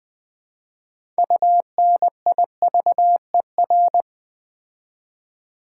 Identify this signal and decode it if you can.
Morse code, heard as a keyed tone, about 20 wpm, 700 Hz. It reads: UNIVER
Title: Build A Word - 3 Letters At A Time - 8-Letter Words - 20wpm